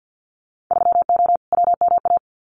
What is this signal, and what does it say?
Morse code, heard as a keyed tone, about 50 wpm, 700 Hz. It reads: 3O WMA
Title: Winter Field Day Exchanges Encore 50wpm